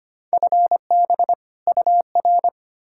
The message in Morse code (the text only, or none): FB UR